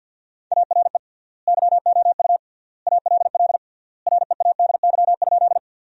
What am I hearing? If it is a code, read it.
Morse code, heard as a keyed tone, about 50 wpm, 700 Hz. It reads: ARE YOU ALL READY?
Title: 3x Word-Spacing - Sentences from Top 400 Words - 50wpm